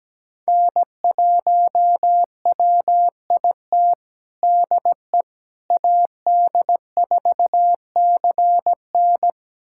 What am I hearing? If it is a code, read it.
Morse code, heard as a keyed tone, about 17 wpm, 700 Hz. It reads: N1WIT DE AD4CN
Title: USA Callsign DE Callsign - 5 Characters - 17wpm